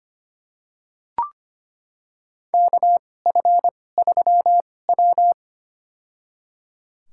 Morse code, heard as a keyed tone, about 25 wpm, spaced as for 18 wpm, 700 Hz. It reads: KF3W
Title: USA Call Signs - 4 Characters - 2x Character Spacing - 25wpm